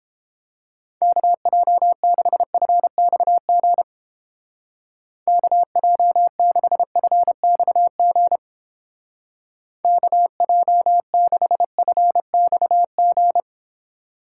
Morse code, heard as a keyed tone, about 30 wpm, 700 Hz. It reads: KJ6FXG KJ6FXG KJ6FXG